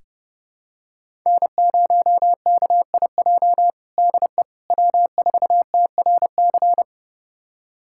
Morse code, heard as a keyed tone, about 32 wpm, 700 Hz. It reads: N0KIJ DE W4TRC
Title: USA Callsign DE Callsign - 5 Characters - 30wpm